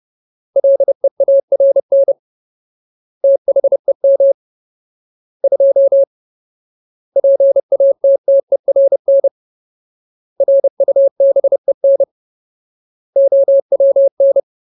Morse code, heard as a keyed tone, about 30 wpm, 550 Hz. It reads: LEARN THEM 2 PATTERN RUBEN OWN
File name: Ultimate Word List / 30WPM Ultimate